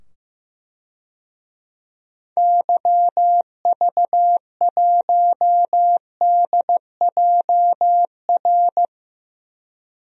Morse code, heard as a keyed tone, about 15 wpm, 700 Hz. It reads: YV1DJR
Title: DXCC-100 Call Signs - 6 Characters - 15wpm